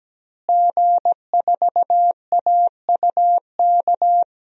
Morse code, heard as a keyed tone, about 17 wpm, 700 Hz. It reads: G4AUK
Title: DXCC-100 Call Signs - 5 Characters - Repeated 3x - 17wpm